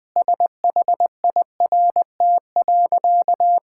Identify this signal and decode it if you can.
Morse code, heard as a keyed tone, about 20 wpm, 700 Hz. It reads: SHIRT.
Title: Easy Sentences with Hints - 20wpm